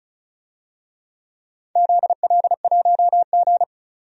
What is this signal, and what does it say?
Morse code, heard as a keyed tone, about 36 wpm, 690 Hz. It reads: ZL1G
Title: DXCC-100 Call Signs - 4 Characters - 35wpm